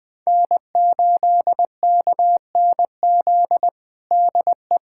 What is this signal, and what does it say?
Morse code, heard as a keyed tone, about 21 wpm, 700 Hz. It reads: N8KNZ DE
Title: USA Callsign DE Callsign - 5 Characters - 20wpm